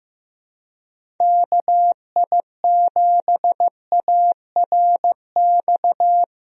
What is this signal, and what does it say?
Morse code, heard as a keyed tone, about 15 wpm, 700 Hz. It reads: KI7ARX
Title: USA Call Signs - 6 Characters - 15wpm